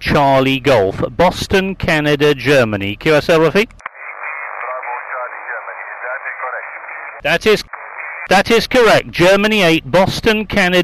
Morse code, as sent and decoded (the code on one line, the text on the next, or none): none